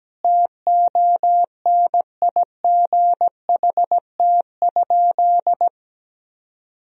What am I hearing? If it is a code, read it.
Morse code, heard as a keyed tone, about 17 wpm, 700 Hz. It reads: TONIGHT?
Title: Sentences from Top 1000 Words - Review Part 2 - 17wpm